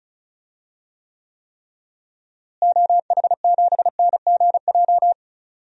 Morse code, heard as a keyed tone, about 36 wpm, 700 Hz. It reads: OH7NGJ
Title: DXCC-100 Call Signs - 6 Characters - Repeated 3x - 35wpm